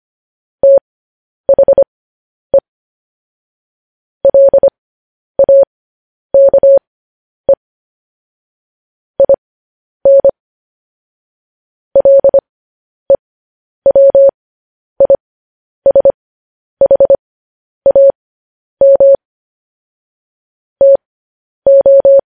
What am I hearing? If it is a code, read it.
Morse code, heard as a keyed tone, about 25 wpm, spaced as for 10 wpm, 550 Hz. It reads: THE LAKE IN LEWISHAM TO
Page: news headlines in morse code at 10 wpm